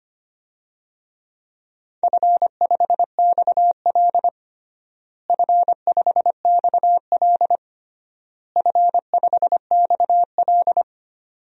Morse code, heard as a keyed tone, about 25 wpm, 700 Hz. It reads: F5XL F5XL F5XL
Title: DXCC-100 Call Signs - 4 Characters - Repeated 3x - 25wpm